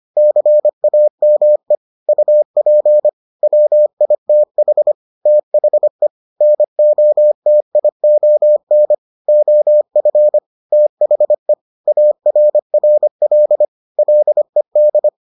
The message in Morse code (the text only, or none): CAME UP WITH THE NOTION OF THE ARRL LED